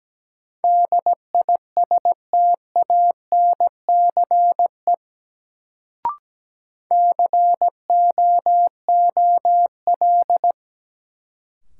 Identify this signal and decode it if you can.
Morse code, heard as a keyed tone, about 17 wpm, 700 Hz. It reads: DISTANCE COOL
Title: Sentences from Top 1200 Words - New Word Drill Alt - 17wpm